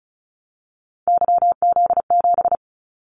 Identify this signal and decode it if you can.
Morse code, heard as a keyed tone, about 34 wpm, 700 Hz. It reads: YZ7
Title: Callsign Prefixes for Top 25 DXCC - 35wpm